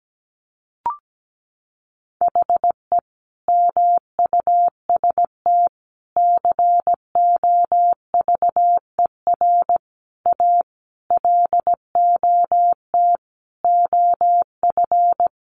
Morse code, heard as a keyed tone, about 17 wpm, 700 Hz. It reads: HE MUST COVER A LOT OF
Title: Sentences from Top 500 Words 17wpm